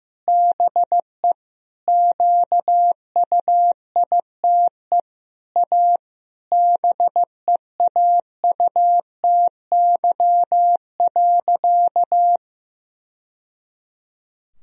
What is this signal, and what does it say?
Morse code, heard as a keyed tone, about 15 wpm, 700 Hz. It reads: BE QUITE A BEAUTY.
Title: Sentences from Top 1000 Words - Review Part 2 - 15wpm